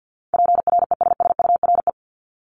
Morse code, heard as a keyed tone, about 50 wpm, 700 Hz. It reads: PRESSURE
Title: Multi-letter Phonemes Repeated 3x 50wpm